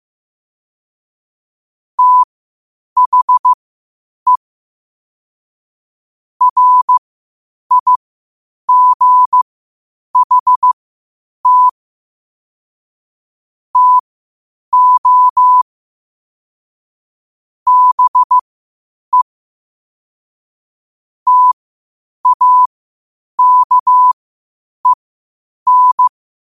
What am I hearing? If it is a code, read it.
Morse code, heard as a keyed tone, about 15 wpm, spaced as for 8 wpm, 1000 Hz. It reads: THE RIGHT TO BE TAKEN
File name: qotdfiles / 8wpm